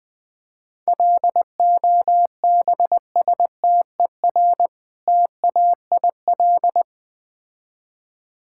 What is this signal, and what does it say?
Morse code, heard as a keyed tone, about 20 wpm, 700 Hz. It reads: LOBSTER TAIL
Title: Difficult Words with Hints - 20wpm